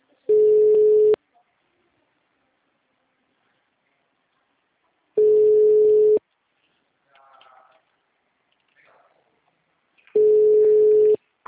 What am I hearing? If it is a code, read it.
Ringback tone (Europe, 425 Hz).